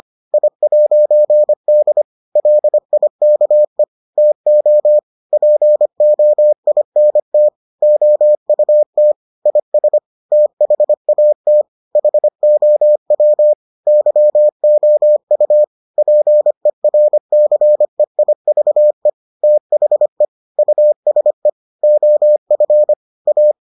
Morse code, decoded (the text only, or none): I'D LIKE TO POINT OUT IS THAT HOW YOU PERCEIVE THE USE OF A